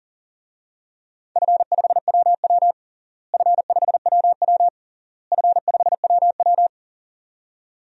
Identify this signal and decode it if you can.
Morse code, heard as a keyed tone, about 40 wpm, 700 Hz. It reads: F5WW F5WW F5WW
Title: DXCC-100 Call Signs - 4 Characters - Repeated 3x - 40wpm